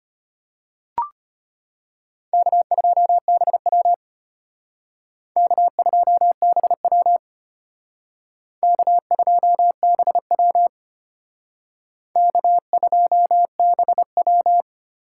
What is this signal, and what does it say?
Morse code, heard as a keyed tone, about 32 wpm, 700 Hz. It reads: K2BW K2BW K2BW K2BW